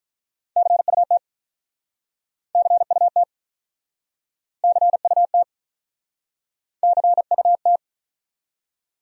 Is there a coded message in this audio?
Yes: Morse code from 0.6 to 7.8 s, heard as a keyed tone, about 47 wpm, 700 Hz. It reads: CUT CUT CUT CUT